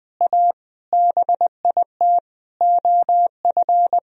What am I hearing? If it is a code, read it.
Morse code, heard as a keyed tone, about 21 wpm, 700 Hz. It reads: A BIT OF
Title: Sentences from Top 1000 Words - Review Part 3 - 20wpm